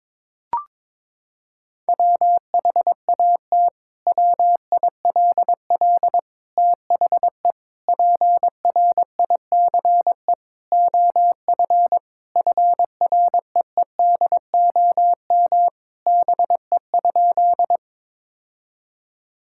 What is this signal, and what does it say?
Morse code, heard as a keyed tone, about 22 wpm, 700 Hz. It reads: WHAT WILL THE PRICE OF FREEDOM BE?
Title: Sentences from Top 1000 Words - Review Part 5 - 22wpm